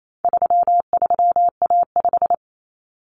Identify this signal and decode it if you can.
Morse code, heard as a keyed tone, about 28 wpm, 700 Hz. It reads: 33A5
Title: Common Cut Numbers - 4-Digit 28wpm